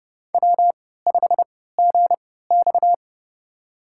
Morse code, heard as a keyed tone, about 30 wpm, spaced as for 17 wpm, 700 Hz. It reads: W5GX
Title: USA Call Signs - 4 Characters - 3x Character Spacing - 30wpm